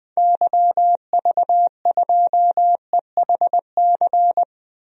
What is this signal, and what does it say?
Morse code, heard as a keyed tone, about 21 wpm, 700 Hz. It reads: YV2EHC
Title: DXCC-100 Call Signs - 6 Characters - 20wpm